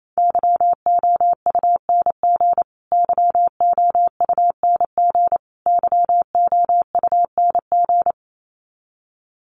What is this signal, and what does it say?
Morse code, heard as a keyed tone, about 28 wpm, 700 Hz. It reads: YOUNG YOUNG YOUNG
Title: Top 500 Words Repeated 28wpm